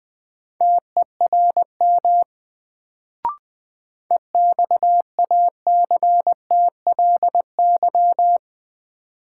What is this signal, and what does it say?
Morse code, heard as a keyed tone, about 20 wpm, 700 Hz. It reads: TERM EXACTLY
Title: Sentences from Top 900 Words - New Word Drill - 20wpm